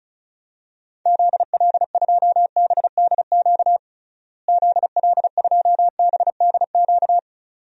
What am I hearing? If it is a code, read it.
Morse code, heard as a keyed tone, about 36 wpm, 700 Hz. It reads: ZL2BDQ ZL2BDQ
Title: DXCC-100 Call Signs - 6 Characters - Repeated 2x - 35wpm